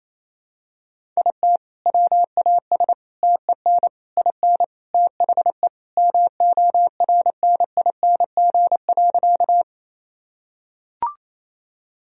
Morse code, heard as a keyed tone, about 27 wpm, 700 Hz. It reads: IT WAS TEN IN THE MORNING.
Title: Sentences from Top 1000 Words - Review Part 2 - 28wpm